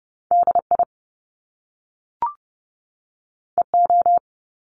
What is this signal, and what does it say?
Morse code, heard as a keyed tone, about 30 wpm, 700 Hz. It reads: DI EO
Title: Top Two Letter Combinations in Words 30wpm